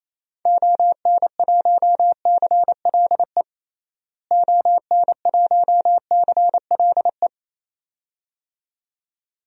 Morse code, heard as a keyed tone, about 27 wpm, 700 Hz. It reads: ON1CLE ON1CLE
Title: DXCC-100 Call Signs - 6 Characters - Repeated 2x - 28wpm